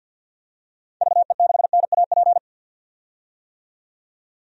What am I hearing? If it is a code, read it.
Morse code, heard as a keyed tone, about 52 wpm, 700 Hz. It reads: VE6NAP
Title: DXCC-100 Call Signs - 6 Characters - 50wpm